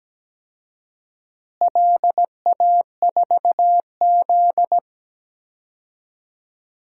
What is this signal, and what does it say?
Morse code, heard as a keyed tone, about 17 wpm, 700 Hz. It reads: LA4Z
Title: DXCC-100 Call Signs - 4 Characters - Repeated 2x - 17wpm